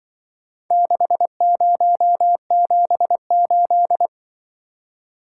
Morse code, heard as a keyed tone, about 24 wpm, 700 Hz. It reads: 6078
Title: Speed-Racing - Four Letter-Number Combinations - 20wpm